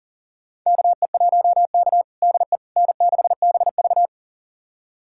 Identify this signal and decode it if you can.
Morse code, heard as a keyed tone, about 37 wpm, 700 Hz. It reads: KE1K DE N6BV
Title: USA Callsign DE Callsign - 4 Characters - 40wpm